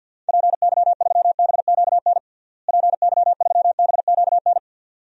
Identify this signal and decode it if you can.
Morse code, heard as a keyed tone, about 52 wpm, 700 Hz. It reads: PY3BQN PY3BQN
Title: DXCC-100 Call Signs - 6 Characters - Repeated 2x - 50wpm